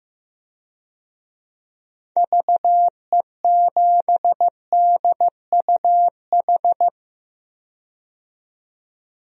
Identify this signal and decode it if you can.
Morse code, heard as a keyed tone, about 15 wpm, 700 Hz. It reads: VE7DUH
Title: DXCC-100 Call Signs - 6 Characters - 15wpm